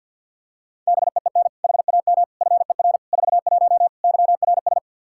Morse code, heard as a keyed tone, about 46 wpm, 700 Hz. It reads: BEEN HAM FER 41 YRS
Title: Practice QSO - Exchange 3 - More Info - 50wpm